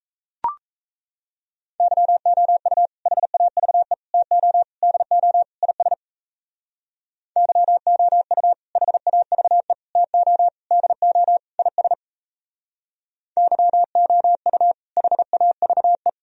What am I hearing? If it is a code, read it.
Morse code, heard as a keyed tone, about 37 wpm, 700 Hz. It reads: YOU HAVE TO DO IS YOU HAVE TO DO IS YOU HAVE